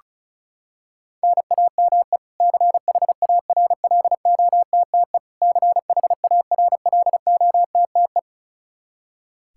Morse code, heard as a keyed tone, about 34 wpm, 700 Hz. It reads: NAME CHARLOTTE CHARLOTTE
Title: Practice QSO - Exchange 1 - Beginning - 35wpm